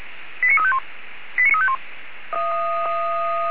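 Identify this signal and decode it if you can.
Five-tone selective call (ZVEI), heard as quick runs of five tones, tones of about 80 ms. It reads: 89351 89351
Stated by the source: ZVEI five-tone selective call received on SDR.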